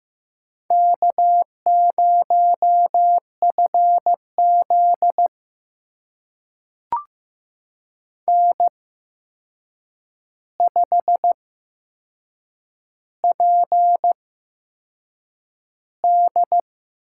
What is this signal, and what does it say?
Morse code, heard as a keyed tone, about 15 wpm, spaced as for 6 wpm, 700 Hz. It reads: K0FZ N5PD
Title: USA Call Signs - 8x Character Spacing - 15wpm